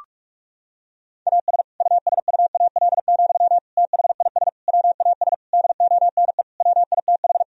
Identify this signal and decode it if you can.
Morse code, heard as a keyed tone, about 46 wpm, 700 Hz. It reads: AS USUAL, THIS WAS DONE WITH
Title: Sentences from Top 1000 Words - Review Part 2 - 45wpm